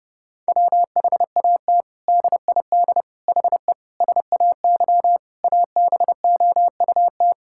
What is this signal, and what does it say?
Morse code, heard as a keyed tone, about 30 wpm, 700 Hz. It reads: WHAT DID HE SAY ABOUT
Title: Sentences from Top 1000 Words - Review Part 6 - 30wpm